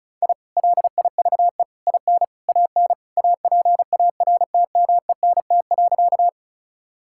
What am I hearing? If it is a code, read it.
Morse code, heard as a keyed tone, about 36 wpm, 700 Hz. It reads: I LIVE IN AN APARTMENT.
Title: Sentences from Top 1000 Words - Review Part 6 - 35wpm